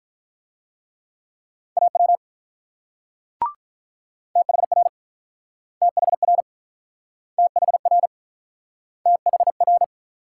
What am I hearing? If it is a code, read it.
Morse code, heard as a keyed tone, about 48 wpm, 700 Hz. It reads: AW THR THR THR THR